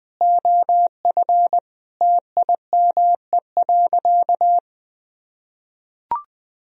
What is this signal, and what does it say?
Morse code, heard as a keyed tone, about 20 wpm, 700 Hz. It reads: OF TIME.
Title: Sentences from Top 1000 Words - Review Part 3 - 20wpm